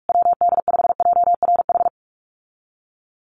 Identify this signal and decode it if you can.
Morse code, heard as a keyed tone, about 44 wpm, 700 Hz. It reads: WD5JRH
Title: USA Call Signs Repeated 2x 45wpm